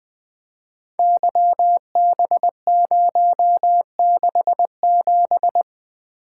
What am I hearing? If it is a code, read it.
Morse code, heard as a keyed tone, about 20 wpm, 700 Hz. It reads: YB067
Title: Code Groups - 5 Letter:Numbers - 20wpm